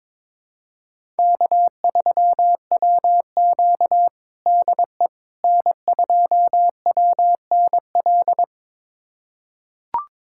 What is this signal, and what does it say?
Morse code, heard as a keyed tone, about 22 wpm, 700 Hz. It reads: K3WQ DE N2WNL
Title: USA Callsign DE Callsign 22wpm